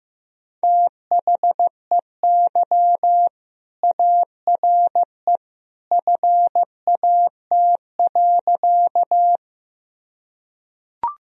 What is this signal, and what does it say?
Morse code, heard as a keyed tone, about 15 wpm, 700 Hz. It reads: THEY ARE FAT.
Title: Sentences from Top 1000 Words - Review Part 4 - 15wpm